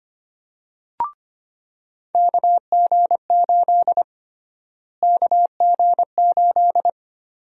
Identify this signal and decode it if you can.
Morse code, heard as a keyed tone, about 25 wpm, 700 Hz. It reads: KG8 KG8
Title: USA Call Sign Prefixes - Repeated 2x - 25wpm